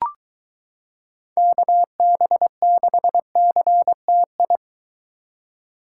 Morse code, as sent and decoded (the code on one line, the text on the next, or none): -.- -... -.... -.-. - ..
KB6CTI